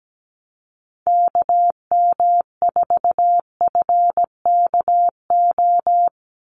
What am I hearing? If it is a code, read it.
Morse code, heard as a keyed tone, about 17 wpm, 700 Hz. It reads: KM4FKO
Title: USA Call Signs 17wpm